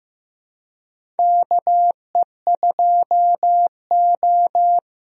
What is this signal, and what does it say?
Morse code, heard as a keyed tone, about 15 wpm, 700 Hz. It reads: KE2O
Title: USA Call Signs - 4 Characters - Repeated 2x - 15wpm